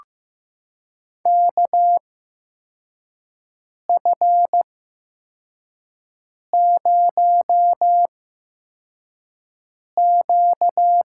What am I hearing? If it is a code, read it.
Morse code, heard as a keyed tone, about 15 wpm, spaced as for 4 wpm, 700 Hz. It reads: KF0Q
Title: USA Call Signs - 5 Characters - 8x Character Spacing - 15wpm